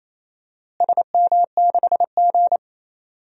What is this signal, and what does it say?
Morse code, heard as a keyed tone, about 28 wpm, 700 Hz. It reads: SM6G
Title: DXCC-100 Call Signs - 4 Characters - Repeated 3x - 28wpm